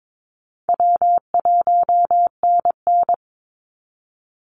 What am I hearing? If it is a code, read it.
Morse code, heard as a keyed tone, about 22 wpm, 700 Hz. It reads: W1NN
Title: Call Signs Repeated 3x - 22wpm